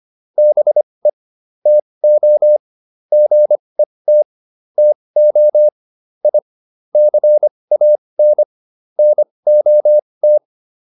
Morse code, heard as a keyed tone, about 25 wpm, spaced as for 20 wpm, 600 Hz. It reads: BE TO GET TO I CAN NOT